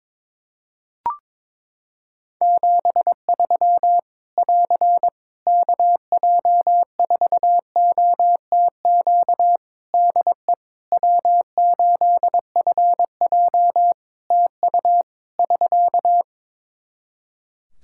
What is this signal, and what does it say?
Morse code, heard as a keyed tone, about 22 wpm, 700 Hz. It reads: 73 <AR> KJ4OTQ DE W8FJ TU <SK>
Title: Practice QSO - Exchange 4 - Ending - 22wpm